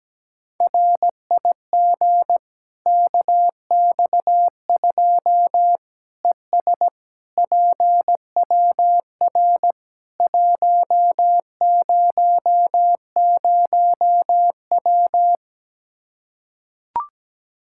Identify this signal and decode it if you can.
Morse code, heard as a keyed tone, about 17 wpm, 700 Hz. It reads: RIG KX2 ES PWR 100W
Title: Practice QSO - Exchange 2 - Info - 17wpm